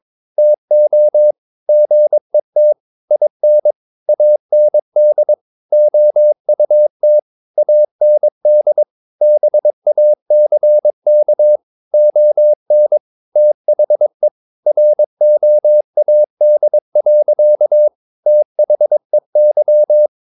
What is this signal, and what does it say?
Morse code, heard as a keyed tone, about 22 wpm, 600 Hz. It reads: TO GET IN AND OUT AND BACK ON THE ROAD. THEY
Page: Romanen Never af Ken Follet på engelsk.